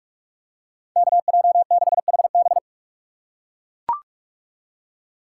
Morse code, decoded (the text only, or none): KJ6HB